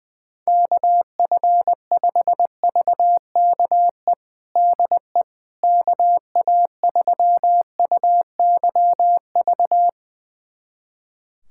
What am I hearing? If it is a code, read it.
Morse code, heard as a keyed tone, about 21 wpm, 700 Hz. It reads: KF5VKE DE KA3UYV
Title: USA Callsign DE Callsign - 6 Characters - 20wpm